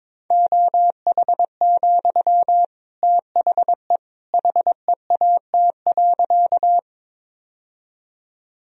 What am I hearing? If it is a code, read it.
Morse code, heard as a keyed tone, about 22 wpm, 700 Hz. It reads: OH, THE HEAT.
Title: Sentences from Top 1000 Words - Review Part 3 - 22wpm